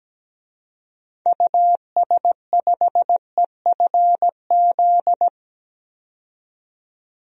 Morse code, heard as a keyed tone, about 17 wpm, 700 Hz. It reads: US5EFZ
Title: DXCC-100 Call Signs - 6 Characters - 17wpm